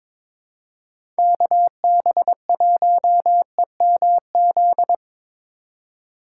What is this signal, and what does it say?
Morse code, heard as a keyed tone, about 21 wpm, 700 Hz. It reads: KB1EMZ
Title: USA Call Signs - 6 Characters - 3x Character Spacing - 22wpm